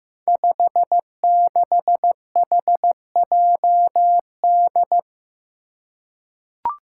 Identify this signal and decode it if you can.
Morse code, heard as a keyed tone, about 15 wpm, 700 Hz. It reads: 56HJD
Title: Five Challenging Letter-Number Combinations 15wpm